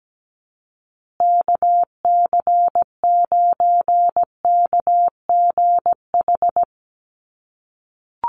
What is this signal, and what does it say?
Morse code, heard as a keyed tone, about 17 wpm, 700 Hz. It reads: KC9KGH
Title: USA Call Signs Repeated 3x 17wpm